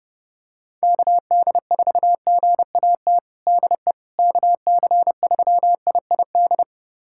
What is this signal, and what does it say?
Morse code, heard as a keyed tone, about 32 wpm, 700 Hz. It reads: KD4GAT DE KC3IID
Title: USA Callsign DE Callsign - 6 Characters - 30wpm